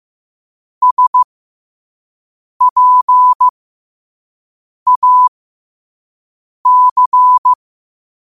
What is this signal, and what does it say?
Morse code, heard as a keyed tone, about 15 wpm, spaced as for 5 wpm, 1000 Hz. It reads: SPAC
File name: qotdfiles / 5wpm